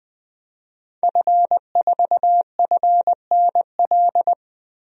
Morse code, heard as a keyed tone, about 20 wpm, 700 Hz. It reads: F4FNL
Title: DXCC-100 Call Signs - 5 Characters - Repeated 3x - 20wpm